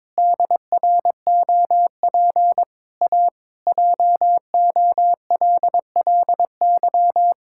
Morse code, heard as a keyed tone, about 21 wpm, 700 Hz. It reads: DROP A JOLLY
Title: Pangram Challenge v2 - 22wpm